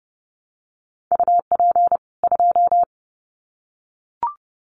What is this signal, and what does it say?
Morse code, heard as a keyed tone, about 30 wpm, 700 Hz. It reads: UP 2
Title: Common QSO Phrases 30wpm